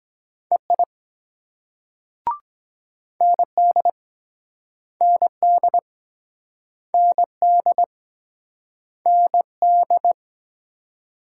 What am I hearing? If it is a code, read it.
Morse code, heard as a keyed tone, about 23 wpm, 700 Hz. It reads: EI ND ND ND ND